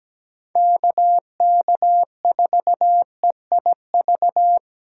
Morse code, heard as a keyed tone, about 20 wpm, 700 Hz. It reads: KK4EIV
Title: Speed Racing - USA 2x3 Call Signs 17wpm